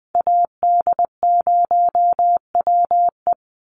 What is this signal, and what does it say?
Morse code, heard as a keyed tone, about 20 wpm, 700 Hz. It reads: AD0WE
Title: Call Signs - 20wpm